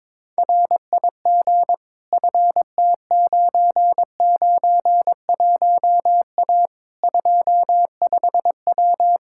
Morse code, heard as a keyed tone, about 22 wpm, 700 Hz. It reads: RIG FT991A 25W
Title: K1USN SST QSO Practice 22wpm